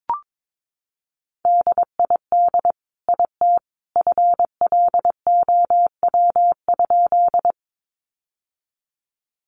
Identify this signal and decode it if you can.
Morse code, heard as a keyed tone, about 22 wpm, 700 Hz. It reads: DID IT FLOW?
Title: Sentences from Top 1200 Words 22wpm